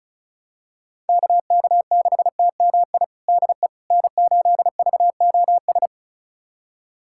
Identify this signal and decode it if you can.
Morse code, heard as a keyed tone, about 36 wpm, 700 Hz. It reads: KK6TMI DE N8VOS
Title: USA Callsign DE Callsign 35wpm